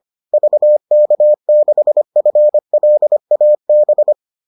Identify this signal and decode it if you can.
Morse code, heard as a keyed tone, about 25 wpm, 600 Hz. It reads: VK6FLAB